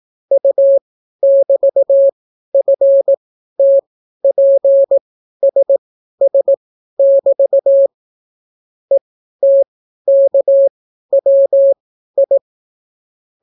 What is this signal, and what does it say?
Morse code, heard as a keyed tone, about 18 wpm, spaced as for 12 wpm, 550 Hz. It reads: U<BT>FTPSS<BT> ETKWI